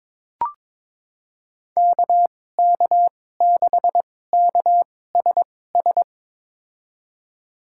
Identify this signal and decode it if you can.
Morse code, heard as a keyed tone, about 22 wpm, spaced as for 16 wpm, 700 Hz. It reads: KK6KSS
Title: USA Call Signs - 6 Characters - 2x Character Spacing - 22wpm